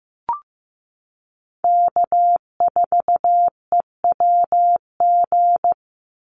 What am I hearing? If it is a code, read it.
Morse code, heard as a keyed tone, about 15 wpm, 700 Hz. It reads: K4EWG
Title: Call Signs Repeated 2x - 15wpm